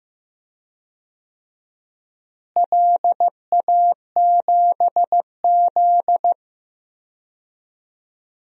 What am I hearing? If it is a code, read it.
Morse code, heard as a keyed tone, about 15 wpm, 700 Hz. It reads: LA7Z
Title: DXCC-100 Call Signs - 4 Characters - Repeated 3x - 15wpm